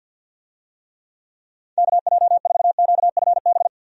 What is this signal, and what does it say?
Morse code, heard as a keyed tone, about 52 wpm, 700 Hz. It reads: KJ4QFB
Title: USA Call Signs - 6 Characters - 50wpm